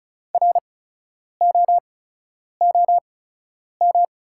Morse code, heard as a keyed tone, about 36 wpm, spaced as for 10 wpm, 690 Hz. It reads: ROOM
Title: 8x Character Spacing - Top 300 Words - 35wpm